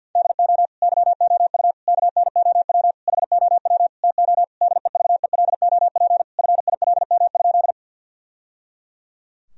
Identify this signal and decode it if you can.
Morse code, heard as a keyed tone, about 50 wpm, 700 Hz. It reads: DO YOU KNOW HOW TO DEVELOP FILM?
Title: Sentences from Top 1000 Words - Review Part 3 - 50wpm